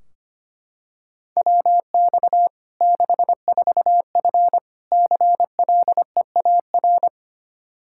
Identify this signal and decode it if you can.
Morse code, heard as a keyed tone, about 25 wpm, 700 Hz. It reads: WX 64F CLEAR
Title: K1USN SST QSO Practice 25wpm